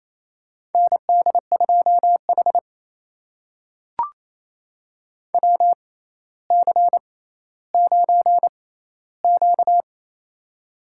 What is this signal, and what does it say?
Morse code, heard as a keyed tone, about 28 wpm, spaced as for 14 wpm, 700 Hz. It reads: ND2H WC9Q
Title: USA Call Signs - 4 Characters - 6x Character Spacing - 28wpm